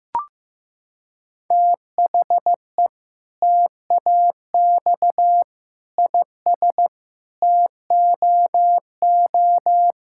Morse code, heard as a keyed tone, about 15 wpm, 700 Hz. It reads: THE TAX IS TOO
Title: Sentences from Top 1000 Words - Review Part 5 - 15wpm